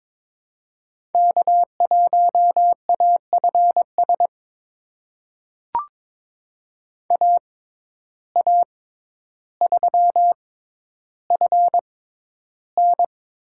Morse code, heard as a keyed tone, about 22 wpm, spaced as for 11 wpm, 700 Hz. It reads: K1AFS AA3FN
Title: USA Call Signs - 5 Characters - 6x Character Spacing - 22wpm